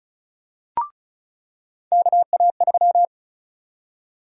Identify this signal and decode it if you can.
Morse code, heard as a keyed tone, about 36 wpm, 700 Hz. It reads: KA3
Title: USA Call Sign Prefixes - 35wpm